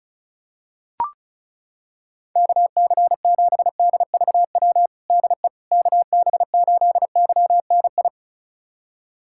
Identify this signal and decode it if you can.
Morse code, heard as a keyed tone, about 34 wpm, 700 Hz. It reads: KC7DVW DE KB8YNI
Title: USA Callsign DE Callsign 35wpm